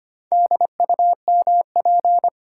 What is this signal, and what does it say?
Morse code, heard as a keyed tone, about 25 wpm, 700 Hz. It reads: DUMP
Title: Sentences from Top 1300 Words - New Word Drill - 25wpm